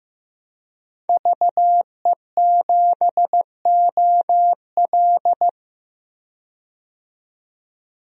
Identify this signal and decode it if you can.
Morse code, heard as a keyed tone, about 15 wpm, 700 Hz. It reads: VE7OL
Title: DXCC-100 Call Signs - 5 Characters - 15wpm